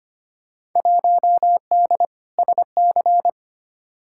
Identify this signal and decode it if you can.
Morse code, heard as a keyed tone, about 25 wpm, 700 Hz. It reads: 1D SC
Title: Field Day Exchanges - 25wpm